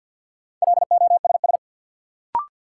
Morse code, heard as a keyed tone, about 51 wpm, 700 Hz. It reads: LOSS
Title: Sentences from Top 1400 Words - New Word Drill Alt - 50wpm